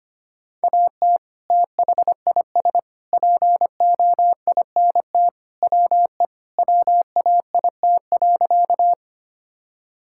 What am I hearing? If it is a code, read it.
Morse code, heard as a keyed tone, about 25 wpm, 700 Hz. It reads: AT THIS POINT WE WAIT.
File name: practice / 25wpm / Sentences with 3-Word Phrases - 25wpm